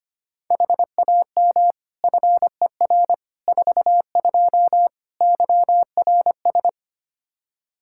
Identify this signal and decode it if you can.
Morse code, heard as a keyed tone, about 25 wpm, 700 Hz. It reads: HAM FER 42 YRS
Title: Practice QSO - Exchange 3 - More Info - 25wpm